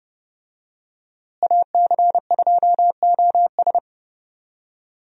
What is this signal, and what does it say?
Morse code, heard as a keyed tone, about 30 wpm, 700 Hz. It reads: AC2OS
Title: USA Call Signs - 5 Characters - Repeated 2x - 30wpm